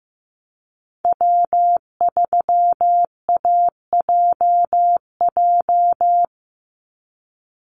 Morse code, heard as a keyed tone, about 15 wpm, 700 Hz. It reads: W3AJJ
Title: USA Call Signs 15wpm